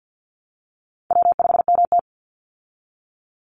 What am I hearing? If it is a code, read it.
Morse code, heard as a keyed tone, about 50 wpm, 700 Hz. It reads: W5MT
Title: Easy Call Signs Repeated 3x - 50wpm